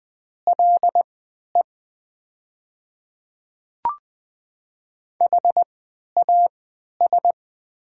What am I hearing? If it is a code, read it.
Morse code, heard as a keyed tone, about 20 wpm, spaced as for 11 wpm, 700 Hz. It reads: LE HAS
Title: 3x Character Spacing - Top 100 Words - 20wpm